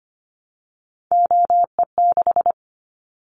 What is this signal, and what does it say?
Morse code, heard as a keyed tone, about 25 wpm, 700 Hz. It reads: OE6
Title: Callsign Prefixes for Top 25 DXCC - 25wpm